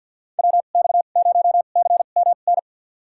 Morse code, heard as a keyed tone, about 52 wpm, spaced as for 37 wpm, 700 Hz. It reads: WX0CMN
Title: USA Call Signs - 2x Character Spacing - 50wpm